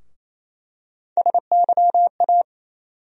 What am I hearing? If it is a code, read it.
Morse code, heard as a keyed tone, about 28 wpm, 700 Hz. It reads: SYA
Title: USA Call Sign Postfixes - 3 Letters 28wpm